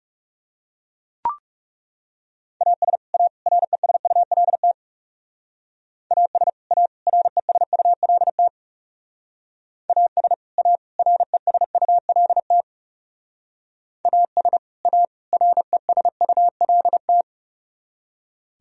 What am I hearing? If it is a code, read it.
Morse code, heard as a keyed tone, about 36 wpm, 700 Hz. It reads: AS A RESULT AS A RESULT AS A RESULT AS A RESULT
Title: Speed-Racing - Sets of 3 Words 30wpm